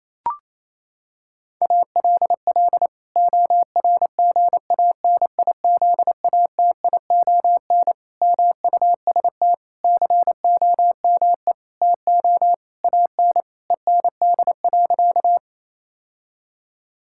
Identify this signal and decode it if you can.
Morse code, heard as a keyed tone, about 28 wpm, 700 Hz. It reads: ALL ORGANIZATION MUST COME TO AN END.
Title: Sentences from Top 1000 Words - Review Part 5 - 28wpm